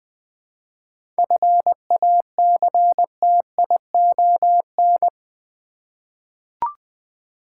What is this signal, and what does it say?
Morse code, heard as a keyed tone, about 20 wpm, 700 Hz. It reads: FACTION
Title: 7 Letter Words - First 3 Letters Given - 20wpm